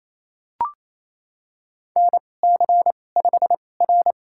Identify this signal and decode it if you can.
Morse code, heard as a keyed tone, about 28 wpm, spaced as for 20 wpm, 700 Hz. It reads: NC5R
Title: USA Call Signs - 4 Characters - 2x Character Spacing - 28wpm